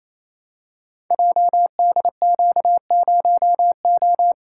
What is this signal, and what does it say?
Morse code, heard as a keyed tone, about 28 wpm, 700 Hz. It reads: JDQ0O
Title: Code Groups - 5 Letter:Numbers - 28wpm